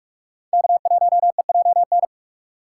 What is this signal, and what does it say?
Morse code, heard as a keyed tone, about 44 wpm, 690 Hz. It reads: K1EJN
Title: USA Call Signs - 5 Characters - 2x Character Spacing - 45wpm